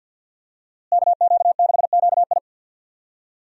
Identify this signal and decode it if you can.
Morse code, heard as a keyed tone, about 51 wpm, 700 Hz. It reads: KQ6QI
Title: USA Call Signs - 5 Characters - Repeated 2x - 50wpm